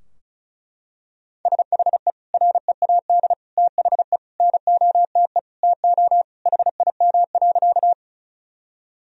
Morse code, heard as a keyed tone, about 34 wpm, 700 Hz. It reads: SHE READ THE NOTE TO HIM.
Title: Sentences from Top 1000 Words - Review Part 2 - 35wpm